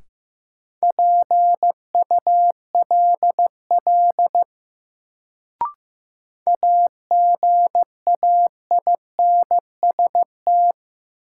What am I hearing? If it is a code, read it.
Morse code, heard as a keyed tone, about 15 wpm, 700 Hz. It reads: PULL AGAINST
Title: Sentences from Top 400 Words - New Word Drill - 15wpm